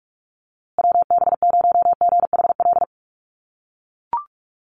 Morse code, heard as a keyed tone, about 44 wpm, 690 Hz. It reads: WB0GHL
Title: USA Call Signs Repeated 2x 45wpm